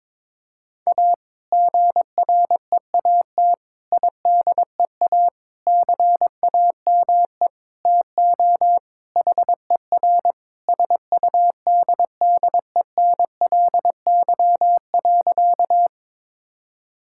Morse code, heard as a keyed tone, about 21 wpm, 700 Hz. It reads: A GREAT IDEA CAME TO HER SUDDENLY.
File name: practice / 22wpm / Sentences with 2-Word Phrases - 22wpm